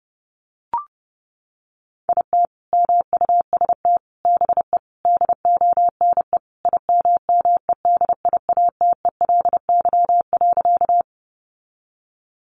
Morse code, heard as a keyed tone, about 30 wpm, 700 Hz. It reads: IT MUST BE DONE IMMEDIATELY.
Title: Sentences from Top 1100 Words 30wpm